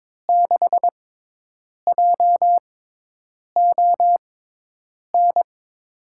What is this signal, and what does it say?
Morse code, heard as a keyed tone, about 22 wpm, spaced as for 8 wpm, 700 Hz. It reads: 6JON
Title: USA Call Signs - 6 Characters - 6x Character Spacing - 22wpm